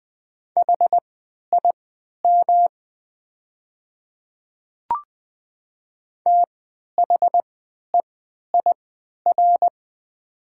Morse code, heard as a keyed tone, about 20 wpm, spaced as for 11 wpm, 700 Hz. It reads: HIM THEIR
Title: 3x Character Spacing - Top 100 Words - 20wpm